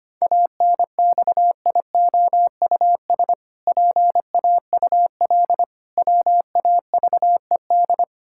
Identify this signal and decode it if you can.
Morse code, heard as a keyed tone, about 25 wpm, 700 Hz. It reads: ANXIOUS PAUL WAVED
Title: Pangram Challenge v2 - 25wpm